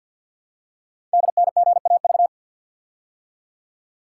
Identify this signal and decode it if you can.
Morse code, heard as a keyed tone, about 48 wpm, 690 Hz. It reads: DNGAV
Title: Code Groups - 5 Letters - 50wpm